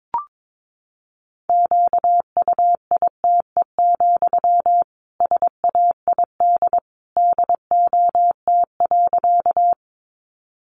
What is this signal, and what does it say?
Morse code, heard as a keyed tone, about 22 wpm, 700 Hz. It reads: QUITE, SAID DOT.
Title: Sentences from Top 1500 Words 22wpm